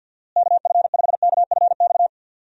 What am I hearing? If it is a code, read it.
Morse code, heard as a keyed tone, about 48 wpm, 700 Hz. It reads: KF5KFX
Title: USA Call Signs - 6 Characters - Repeated 3x - 50wpm